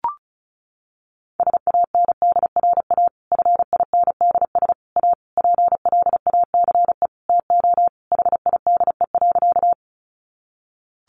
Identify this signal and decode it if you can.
Morse code, heard as a keyed tone, about 34 wpm, 700 Hz. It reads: SANDRA FINDS A PLACE TO HIDE.
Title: Easy Phrases 35wpm